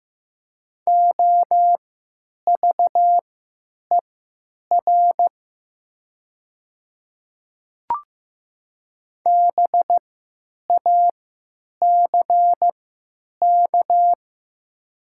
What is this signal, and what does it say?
Morse code, heard as a keyed tone, about 15 wpm, spaced as for 9 wpm, 700 Hz. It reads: OVER BACK
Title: 3x Character Spacing - Top 200 Words - 15wpm